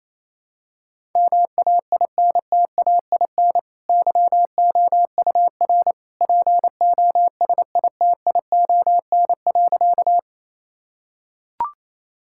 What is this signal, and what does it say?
Morse code, heard as a keyed tone, about 28 wpm, 700 Hz. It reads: MAINTAIN YOUR POSITION.
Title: Sentences from Top 1000 Words - Review Part 5 - 28wpm